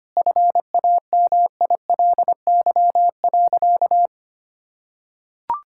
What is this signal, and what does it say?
Morse code, heard as a keyed tone, about 25 wpm, 700 Hz. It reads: FAMILY.
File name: practice / 25wpm / Sentences with 3-Word Phrases - 25wpm